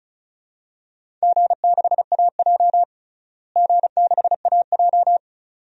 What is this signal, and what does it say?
Morse code, heard as a keyed tone, about 36 wpm, 700 Hz. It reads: G6AJ G6AJ
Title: DXCC-100 Call Signs - 4 Characters - Repeated 2x - 35wpm